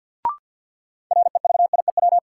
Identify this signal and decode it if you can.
Morse code, heard as a keyed tone, about 51 wpm, 700 Hz. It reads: REVIEW
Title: Sentences from Top 900 Words - New Word Drill - 50wpm